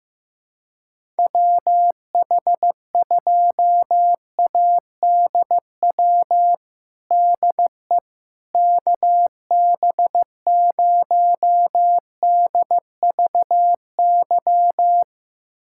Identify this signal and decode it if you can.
Morse code, heard as a keyed tone, about 15 wpm, 700 Hz. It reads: WH2ADW DE KB0DVY
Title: USA Callsign DE Callsign - 6 Characters - 15wpm